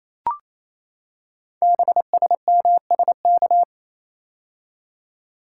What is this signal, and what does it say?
Morse code, heard as a keyed tone, about 27 wpm, 690 Hz. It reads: BSMSK
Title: Code Groups - 5 Letters - 28wpm